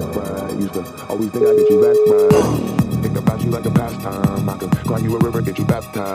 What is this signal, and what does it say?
Ringback tone (Europe, 425 Hz), with speech.